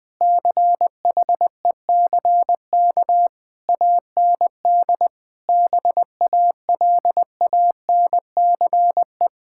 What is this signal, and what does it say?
Morse code, heard as a keyed tone, about 21 wpm, 700 Hz. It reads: CHECK AND BALANCE
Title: Sentences from Top 1000 Words - Review Part 6 - 20wpm